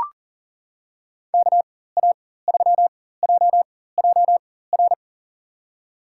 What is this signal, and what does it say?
Morse code, heard as a keyed tone, about 37 wpm, spaced as for 18 wpm, 690 Hz. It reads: KA3JJR
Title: USA Call Signs - 4x Character Spacing - 40wpm